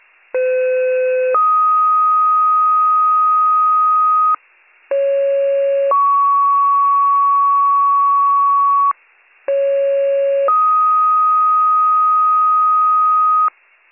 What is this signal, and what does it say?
Two-tone page, heard as a one-second tone, then a three-second tone, about 510 then 1250 Hz, about 550 then 1120 Hz, about 550 then 1250 Hz. It is heard from 0.3 to 13.5 s.